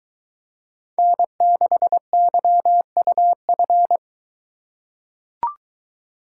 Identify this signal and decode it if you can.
Morse code, heard as a keyed tone, about 23 wpm, 700 Hz. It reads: N6YUF